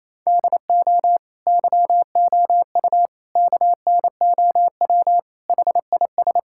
Morse code, heard as a keyed tone, about 27 wpm, 700 Hz. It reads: DO YOU KNOW HIS
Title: Sentences from Top 1600 Words 28wpm